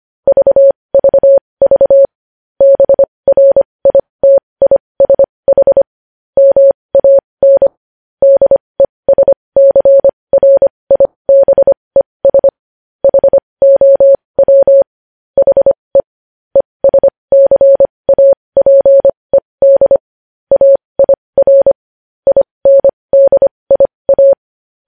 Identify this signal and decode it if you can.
Morse code, heard as a keyed tone, about 25 wpm, spaced as for 20 wpm, 550 Hz. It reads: VVV BRITISH MAN DESCRIBES HOW HE ESCAPED AIR INDIA
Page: News headlines in Morse code at 20 wpm